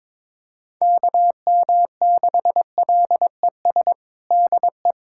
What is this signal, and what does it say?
Morse code, heard as a keyed tone, about 22 wpm, 700 Hz. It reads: KM6LES DE
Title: USA Callsign DE Callsign - 6 Characters - 22wpm